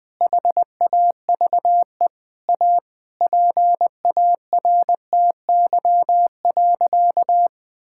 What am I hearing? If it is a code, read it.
Morse code, heard as a keyed tone, about 20 wpm, 700 Hz. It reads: HAVE A PARTY.
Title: Sentences with 4-Word Phrases - 20wpm